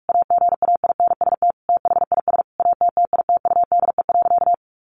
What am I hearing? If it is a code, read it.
Morse code, heard as a keyed tone, about 44 wpm, 700 Hz. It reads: AGAINST THIS ATTITUDE.
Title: Sentences from Top 1300 Words 45wpm